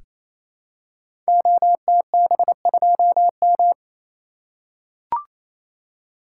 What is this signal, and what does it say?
Morse code, heard as a keyed tone, about 29 wpm, 700 Hz. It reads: OTB2M
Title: Code Groups - 5 Letter:Numbers - 28wpm